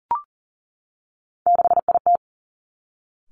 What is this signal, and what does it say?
Morse code, heard as a keyed tone, about 40 wpm, 700 Hz. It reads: 6IT
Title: Three Challenging Letter-Number Combinations 40wpm